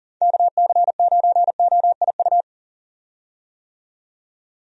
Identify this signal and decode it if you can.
Morse code, heard as a keyed tone, about 40 wpm, 700 Hz. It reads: KC9OIU
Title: USA Call Signs - 6 Characters - 40wpm